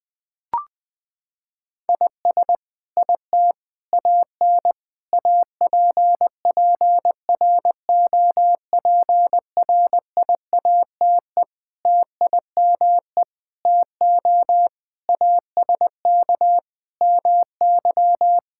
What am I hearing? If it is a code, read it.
Morse code, heard as a keyed tone, about 20 wpm, 700 Hz. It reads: IS IT AN APPROPRIATE TIME TO ASK MY
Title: Sentences from Top 1000 Words - Review Part 5 - 20wpm